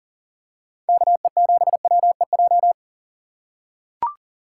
Morse code, heard as a keyed tone, about 40 wpm, 700 Hz. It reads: KE7WEJ
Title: USA Call Signs - 8x Character Spacing - 40wpm